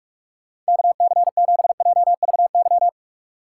Morse code, heard as a keyed tone, about 56 wpm, 700 Hz. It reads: KC7JVY